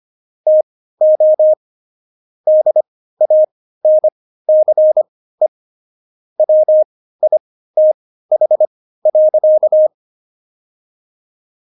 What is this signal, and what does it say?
Morse code, heard as a keyed tone, about 25 wpm, spaced as for 15 wpm, 620 Hz. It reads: TO DANCE WITH.
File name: Jokes-25-char-15-eff